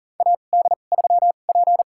Morse code, heard as a keyed tone, about 40 wpm, spaced as for 29 wpm, 700 Hz. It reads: AD3P
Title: USA Call Signs - 2x Character Spacing - 40wpm